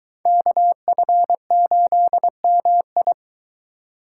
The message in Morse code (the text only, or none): KF8MI